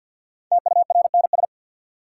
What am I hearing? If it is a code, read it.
Morse code, heard as a keyed tone, about 53 wpm, 700 Hz. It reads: TURNS
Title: Sets of 5 Words Encore - New Word Drill - 50wpm